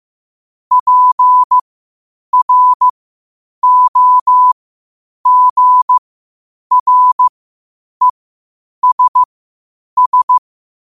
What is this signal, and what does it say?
Morse code, heard as a keyed tone, about 15 wpm, spaced as for 8 wpm, 1000 Hz. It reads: PROGRESS
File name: qotdfiles / 8wpm